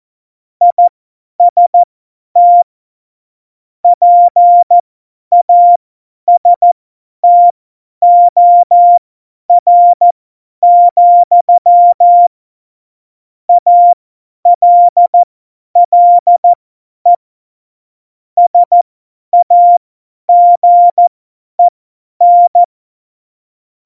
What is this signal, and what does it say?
Morse code, heard as a keyed tone, about 14 wpm, spaced as for 10 wpm, 700 Hz. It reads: IST PASTOR, ALLE SAGEN
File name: Morsekurs / Witze / Witz 27. Oktober 2020, 50 F70